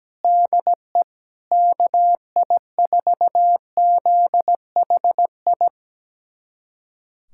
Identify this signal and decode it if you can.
Morse code, heard as a keyed tone, about 17 wpm, 700 Hz. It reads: DE KI4ZHI
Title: USA Callsign DE Callsign - 6 Characters - 17wpm